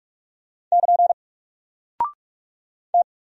Morse code, heard as a keyed tone, about 44 wpm, 700 Hz. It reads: <KN> T
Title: Single Character (Letter:Number:Punctuation:Prosign) 45wpm